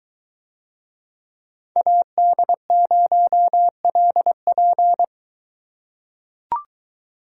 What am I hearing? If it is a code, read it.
Morse code, heard as a keyed tone, about 21 wpm, 700 Hz. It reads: AD0LP